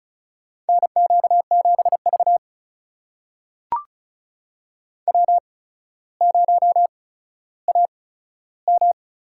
Morse code, heard as a keyed tone, about 36 wpm, spaced as for 15 wpm, 700 Hz. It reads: NQ7V W0AM
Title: USA Call Signs - 8x Character Spacing - 35wpm